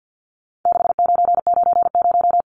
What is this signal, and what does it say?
Morse code, heard as a keyed tone, about 50 wpm, 700 Hz. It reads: 6990
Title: Numbers - 4-Digit 50wpm